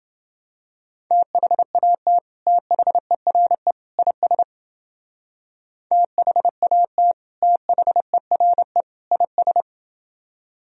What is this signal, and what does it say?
Morse code, heard as a keyed tone, about 30 wpm, 700 Hz. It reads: THAT THERE IS THAT THERE IS